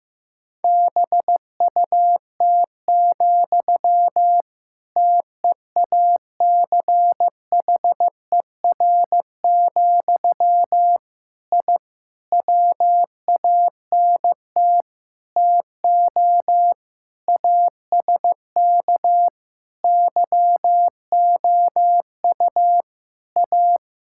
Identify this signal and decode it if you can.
Morse code, heard as a keyed tone, about 15 wpm, 700 Hz. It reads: BUT, TEACHER, I WANT TO ASK YOU A
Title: Sentences from Top 1000 Words - Review Part 6 - 15wpm